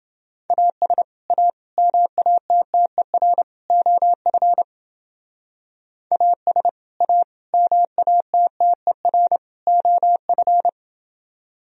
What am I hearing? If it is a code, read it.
Morse code, heard as a keyed tone, about 31 wpm, 700 Hz. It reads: AS A MATTER OF AS A MATTER OF